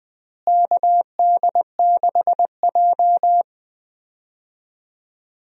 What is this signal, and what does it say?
Morse code, heard as a keyed tone, about 20 wpm, 700 Hz. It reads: KD6J
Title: USA Call Signs - 4 Characters - Repeated 3x - 20wpm